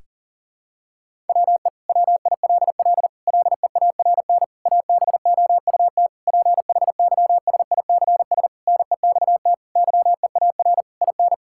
Morse code, heard as a keyed tone, about 40 wpm, 700 Hz. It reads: WE WILL LEARN ABOUT PHYSICS NEXT YEAR IN
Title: Sentences from Top 1000 Words - Review Part 5 - 40wpm